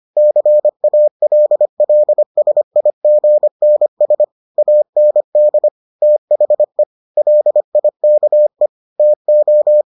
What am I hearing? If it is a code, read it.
Morse code, heard as a keyed tone, about 25 wpm, 600 Hz. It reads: CALLSIGNS AND THE LIKE TO